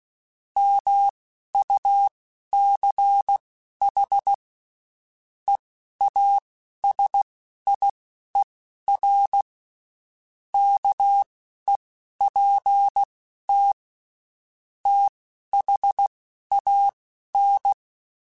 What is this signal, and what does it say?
Morse code, heard as a keyed tone, about 16 wpm, spaced as for 11 wpm, 800 Hz. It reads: MUCH EASIER KEPT THAN
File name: Quotes Spaced / Quotes Spaced-mp3s / Character sp @15